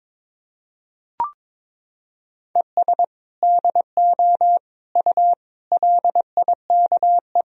Morse code, heard as a keyed tone, about 21 wpm, 700 Hz. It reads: ES DO U LIKE
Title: Sentences from Top 100 Words and Abbreviations 22wpm